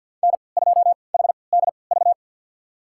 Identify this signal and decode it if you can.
Morse code, heard as a keyed tone, about 53 wpm, spaced as for 29 wpm, 700 Hz. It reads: N2HDV
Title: USA Call Signs - 5 Characters - 3x Character Spacing - 50wpm